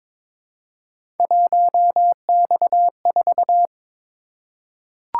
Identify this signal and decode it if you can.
Morse code, heard as a keyed tone, about 22 wpm, 700 Hz. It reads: 1X4
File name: practice / 22wpm / Three Letter-Number Combinations - 22wpm